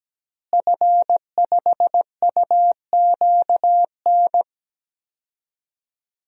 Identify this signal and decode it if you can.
Morse code, heard as a keyed tone, about 17 wpm, 700 Hz. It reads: F5UQN
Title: DXCC-100 Call Signs - 5 Characters - Repeated 3x - 17wpm